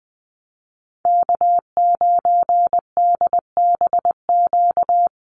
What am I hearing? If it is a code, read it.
Morse code, heard as a keyed tone, about 20 wpm, 700 Hz. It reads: K9DBQ
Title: USA Call Signs Repeated 3x 20wpm